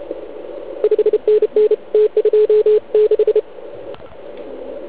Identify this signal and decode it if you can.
Morse code, heard as a keyed tone, about 31 wpm, 430 Hz. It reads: 5NN T26
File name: Archiv / CUC149 / OK1IF CUC149